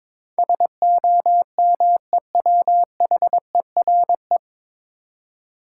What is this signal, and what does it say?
Morse code, heard as a keyed tone, about 22 wpm, 700 Hz. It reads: SOMEWHERE
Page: Morse Code Ninja practice 22wpm